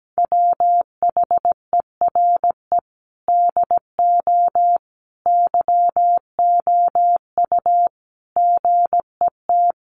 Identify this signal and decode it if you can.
Morse code, heard as a keyed tone, about 17 wpm, 700 Hz. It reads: WHERE DO YOU GET
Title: Sentences from Top 1400 Words 17wpm